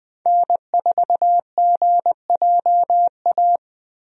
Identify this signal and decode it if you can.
Morse code, heard as a keyed tone, about 20 wpm, 700 Hz. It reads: N4GJA
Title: USA Call Signs - 5 Characters - 20wpm